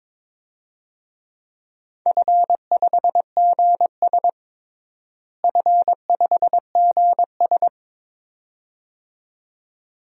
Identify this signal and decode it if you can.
Morse code, heard as a keyed tone, about 22 wpm, 700 Hz. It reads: F5GS F5GS
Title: DXCC-100 Call Signs - 4 Characters - Repeated 2x - 22wpm